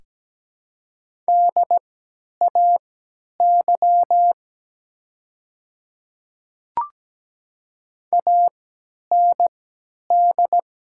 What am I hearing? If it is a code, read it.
Morse code, heard as a keyed tone, about 17 wpm, spaced as for 10 wpm, 700 Hz. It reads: DAY AND
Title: 3x Character Spacing - Top 100 Words - 17wpm